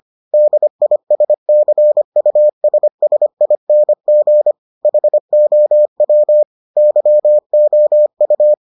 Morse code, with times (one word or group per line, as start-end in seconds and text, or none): 0.3-4.5 DISCUSSING
4.8-6.4 HOW
6.8-8.5 YOU